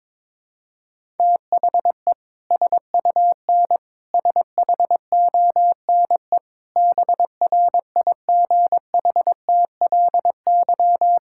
Morse code, heard as a keyed tone, about 23 wpm, 700 Hz. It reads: THE SUN SHONE BRIGHTLY
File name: practice / 22wpm / Sentences with 2-Word Phrases - 22wpm